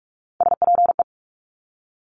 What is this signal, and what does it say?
Morse code, heard as a keyed tone, about 46 wpm, 700 Hz. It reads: SPE
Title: Top Three Letter Combinations in Words 45wpm